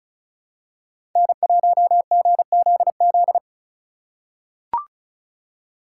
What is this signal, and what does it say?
Morse code, heard as a keyed tone, about 36 wpm, 700 Hz. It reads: N1GZZ
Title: USA Call Signs - 5 Characters - 2x Character Spacing - 35wpm